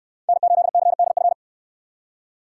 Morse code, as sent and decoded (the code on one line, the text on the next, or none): -. ---.. -.-. -.. ..-.
N8CDF